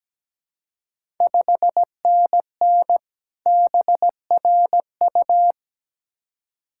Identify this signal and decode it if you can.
Morse code, heard as a keyed tone, about 17 wpm, 700 Hz. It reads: 5NN BRU
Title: Ontario QSO Party Exchange 17wpm